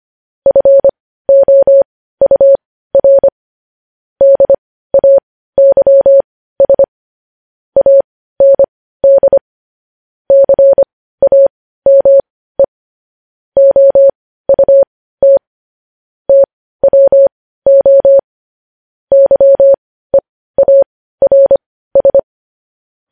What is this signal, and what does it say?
Morse code, heard as a keyed tone, about 25 wpm, spaced as for 15 wpm, 550 Hz. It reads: FOUR DAYS AND CAME OUT TWO YEARS
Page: news headlines in morse code at 15 wpm